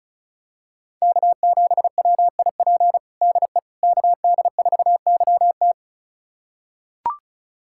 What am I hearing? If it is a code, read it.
Morse code, heard as a keyed tone, about 34 wpm, 690 Hz. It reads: K7WIP DE KD4YT
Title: USA Callsign DE Callsign - 5 Characters - 35wpm